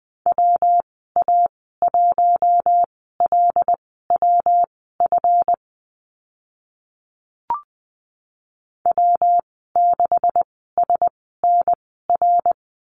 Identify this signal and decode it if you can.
Morse code, heard as a keyed tone, about 20 wpm, spaced as for 15 wpm, 700 Hz. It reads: WA1LWF W6SNR
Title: 140 Lesson 23g - Letter f - Call Signs 20wpm